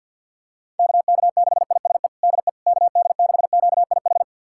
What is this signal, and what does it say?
Morse code, heard as a keyed tone, about 52 wpm, 700 Hz. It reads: KK6ISE DE KD6QIH
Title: USA Callsign DE Callsign - 6 Characters - 50wpm